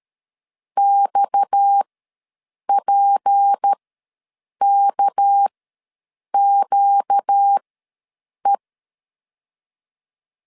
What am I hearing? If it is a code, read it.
Morse code, heard as a keyed tone, about 13 wpm, spaced as for 7 wpm, 800 Hz. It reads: XPKQE
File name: LEZIONI / 6 PKX0 / 6 PKX ws 12wpm 3106